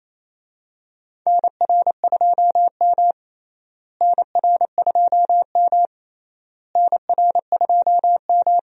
Morse code, heard as a keyed tone, about 27 wpm, 700 Hz. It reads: NR2M NR2M NR2M
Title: USA Call Signs - 4 Characters - Repeated 3x - 28wpm